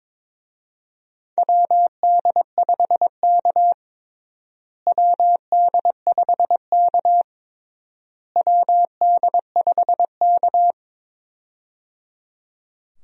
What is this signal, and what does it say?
Morse code, heard as a keyed tone, about 21 wpm, 700 Hz. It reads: WD5K WD5K WD5K
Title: USA Call Signs - 4 Characters - Repeated 3x - 22wpm